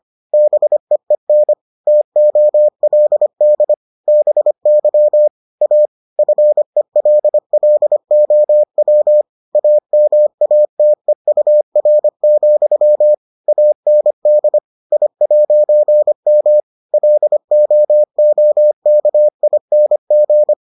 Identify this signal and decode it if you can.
Morse code, heard as a keyed tone, about 25 wpm, 600 Hz. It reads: BEEN TOLD BY A FELLOW AMATEUR, AND I'M LOOKING